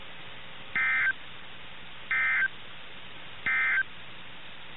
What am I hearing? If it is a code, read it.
Emergency Alert System (SAME) tones, heard as harsh buzzing bursts. It is heard from 0.6 to 3.9 s.
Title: NOAA Weather Radio - KIH61